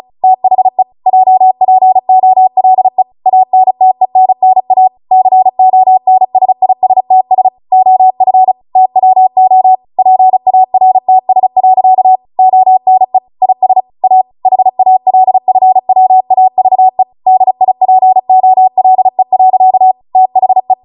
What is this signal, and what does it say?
Morse code, heard as a keyed tone, about 36 wpm, 750 Hz. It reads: THE JPOLE ANTENNA CONSISTS OF TWO PARTS. ONE IS A HALFWAVE DIPOLE. THE